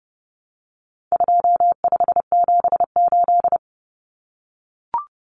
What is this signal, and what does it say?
Morse code, heard as a keyed tone, about 30 wpm, 700 Hz. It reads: 2578
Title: Common Cut Numbers - 4-Digit 30wpm